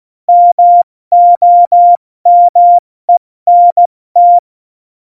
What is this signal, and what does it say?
Morse code, heard as a keyed tone, about 16 wpm, 700 Hz. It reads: MOMENT
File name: Morsekurs / Witze / Witz 15. Januar 2019, 70 F80